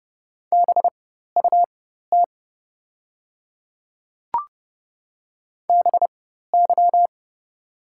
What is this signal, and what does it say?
Morse code, heard as a keyed tone, about 30 wpm, spaced as for 14 wpm, 700 Hz. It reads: BUT BY
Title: 4x Character Spacing - Top 200 Words - 30wpm